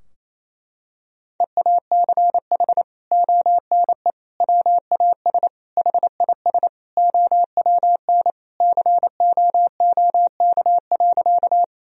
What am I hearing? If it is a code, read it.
Morse code, heard as a keyed tone, about 28 wpm, 700 Hz. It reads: EACH ONE WAS HIS OWN COOK.
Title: Sentences from Top 1000 Words - Review Part 4 - 28wpm